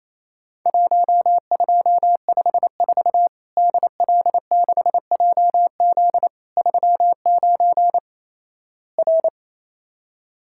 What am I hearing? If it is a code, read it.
Morse code, heard as a keyed tone, about 28 wpm, 700 Hz. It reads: 1254 DL6JZ 39 R
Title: Worked All Europe DX Contest - QTC Exchange - 28wpm